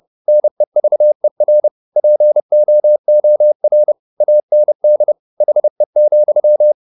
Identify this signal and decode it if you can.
Morse code, heard as a keyed tone, about 30 wpm, 600 Hz. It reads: NEVER POOR AND HE,
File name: the old man and the sea / the old man and the sea 1 30wpm